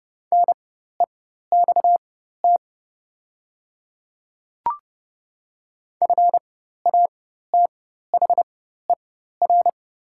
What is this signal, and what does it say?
Morse code, heard as a keyed tone, about 30 wpm, spaced as for 14 wpm, 700 Hz. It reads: NEXT FATHER
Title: 4x Character Spacing - Top 300 Words - 30wpm